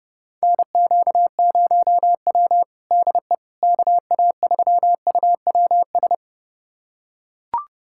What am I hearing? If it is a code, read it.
Morse code, heard as a keyed tone, about 32 wpm, 700 Hz. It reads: NQ0W DE KA3UWS
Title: USA Callsign DE Callsign 30wpm